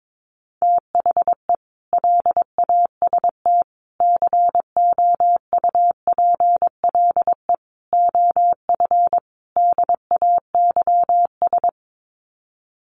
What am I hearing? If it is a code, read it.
Morse code, heard as a keyed tone, about 22 wpm, 700 Hz. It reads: THE LAST COUPLE OF DAYS
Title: Sets of 5 Words Encore 22wpm